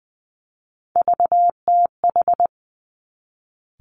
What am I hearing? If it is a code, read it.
Morse code, heard as a keyed tone, about 20 wpm, 700 Hz. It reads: VTH
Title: Three Challenging Letter-Number Combinations 20wpm